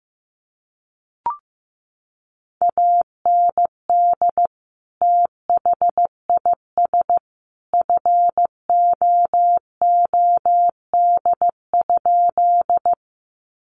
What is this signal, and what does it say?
Morse code, heard as a keyed tone, about 15 wpm, 700 Hz. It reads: AND THIS FOOD?
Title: Sentences from Top 300 Words 15wpm